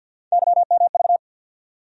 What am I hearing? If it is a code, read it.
Morse code, heard as a keyed tone, about 54 wpm, 700 Hz. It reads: YMV
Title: USA Call Sign Postfixes - 3 Letters 50wpm